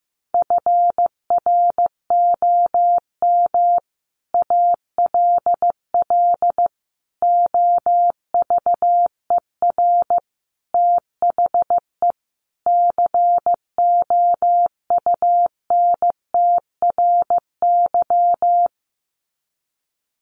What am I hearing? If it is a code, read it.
Morse code, heard as a keyed tone, about 14 wpm, 700 Hz. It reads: FROM ALL OVER THE COUNTRY
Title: Sets of 5 Words Encore 15wpm